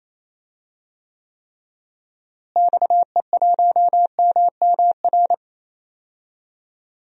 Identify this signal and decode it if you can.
Morse code, heard as a keyed tone, about 28 wpm, 700 Hz. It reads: XE1MMR
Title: DXCC-100 Call Signs - 6 Characters - Repeated 3x - 28wpm